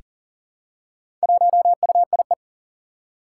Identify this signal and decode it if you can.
Morse code, heard as a keyed tone, about 38 wpm, 700 Hz. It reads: 1UIE
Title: Four Challenging Letter-Number Combinations 40wpm